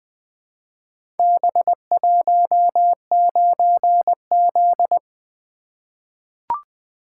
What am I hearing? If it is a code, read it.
Morse code, heard as a keyed tone, about 20 wpm, 700 Hz. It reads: B19Z
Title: Four Letter-Number Combinations - 20wpm